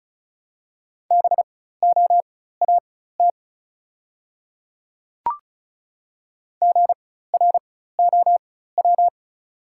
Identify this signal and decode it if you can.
Morse code, heard as a keyed tone, about 36 wpm, spaced as for 16 wpm, 700 Hz. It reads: BOAT GROW
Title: 4x Character Spacing - Top 500 Words - 35wpm